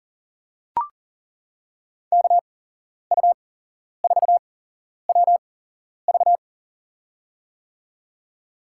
Morse code, heard as a keyed tone, about 40 wpm, spaced as for 11 wpm, 700 Hz. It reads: KU4WV
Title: USA Call Signs - 5 Characters - 8x Character Spacing - 40wpm